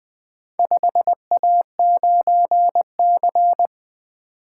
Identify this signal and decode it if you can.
Morse code, heard as a keyed tone, about 21 wpm, 700 Hz. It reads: 5A9C
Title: Four Letter-Number Combinations - 20wpm